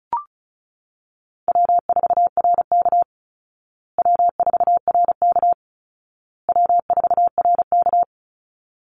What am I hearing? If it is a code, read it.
Morse code, heard as a keyed tone, about 34 wpm, 700 Hz. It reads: W4RK W4RK W4RK
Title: Easy Call Signs Repeated 3x - 35wpm